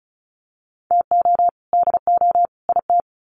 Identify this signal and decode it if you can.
Morse code, heard as a keyed tone, about 34 wpm, 700 Hz. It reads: TO DO IT
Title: Sets of 3 Words 35wpm